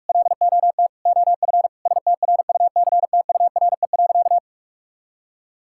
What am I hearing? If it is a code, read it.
Morse code, heard as a keyed tone, about 41 wpm, 700 Hz. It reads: LOT OF STRUCTURE.
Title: Sentences from Top 1000 Words - Review Part 6 - 45wpm